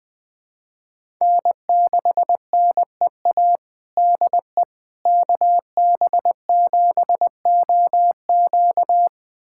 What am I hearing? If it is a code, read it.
Morse code, heard as a keyed tone, about 21 wpm, 700 Hz. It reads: N6NEA DE KB7OQ
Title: USA Callsign DE Callsign - 5 Characters - 20wpm